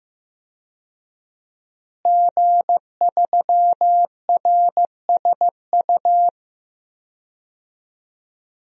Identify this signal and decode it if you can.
Morse code, heard as a keyed tone, about 15 wpm, 700 Hz. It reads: G3RSU
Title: DXCC-100 Call Signs - 5 Characters - Repeated 2x - 15wpm